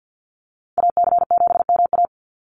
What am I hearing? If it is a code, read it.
Morse code, heard as a keyed tone, about 50 wpm, 700 Hz. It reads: AC7MA
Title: NA SOTA Activator Callsigns 50wpm